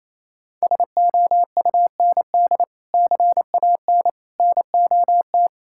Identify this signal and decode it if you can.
Morse code, heard as a keyed tone, about 28 wpm, 700 Hz. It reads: SOUND CAN NOT
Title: Sentences from Top 1000 Words - Review Part 6 - 28wpm